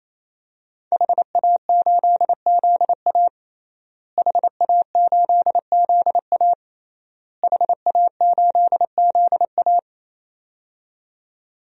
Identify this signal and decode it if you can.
Morse code, heard as a keyed tone, about 28 wpm, 700 Hz. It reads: HA8ZA HA8ZA HA8ZA
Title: DXCC-100 Call Signs - 5 Characters - Repeated 3x - 28wpm